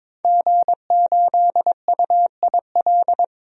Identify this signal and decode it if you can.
Morse code, heard as a keyed tone, about 21 wpm, 700 Hz. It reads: G8UIL
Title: DXCC-100 Call Signs - 5 Characters - Repeated 3x - 22wpm